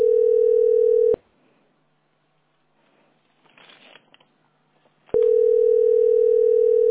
Ringback tone (North America), 440 + 480 Hz.
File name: 08ade5ae-4463-443d-8f33